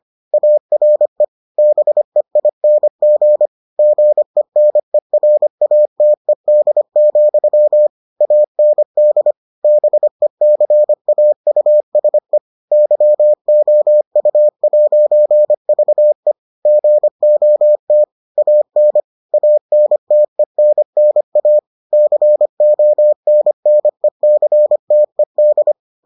Morse code, heard as a keyed tone, about 25 wpm, 600 Hz. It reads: ARE BEING GENERATED, AND BECAUSE YOU'VE GOT AN ANTENNA CONNECTED